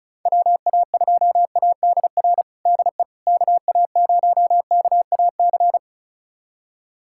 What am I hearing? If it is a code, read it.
Morse code, heard as a keyed tone, about 36 wpm, 690 Hz. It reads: WA2ADR DE KA0KAC
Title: USA Callsign DE Callsign - 6 Characters - 35wpm